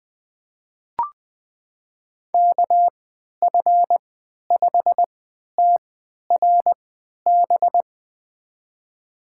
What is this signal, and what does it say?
Morse code, heard as a keyed tone, about 20 wpm, spaced as for 11 wpm, 700 Hz. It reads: KF5TRB
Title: USA Call Signs - 6 Characters - 3x Character Spacing - 20wpm